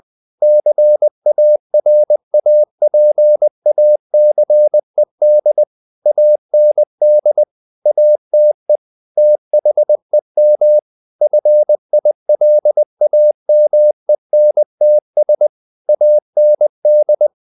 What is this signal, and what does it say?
Morse code, heard as a keyed tone, about 20 wpm, 600 Hz. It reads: CARAPACED AND ATE THEM FILAMENTS AND
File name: the old man and the sea / the old man and the sea part 1 20wpm